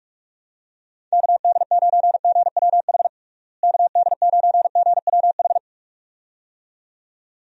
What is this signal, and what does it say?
Morse code, heard as a keyed tone, about 46 wpm, 700 Hz. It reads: KD9GWH KD9GWH
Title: USA Call Signs - 6 Characters - Repeated 2x - 45wpm